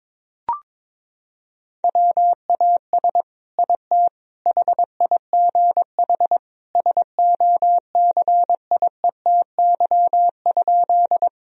Morse code, heard as a keyed tone, about 22 wpm, 700 Hz. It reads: WAS IT HIGH SOCIETY?
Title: Sentences from Top 1000 Words - Review Part 4 - 22wpm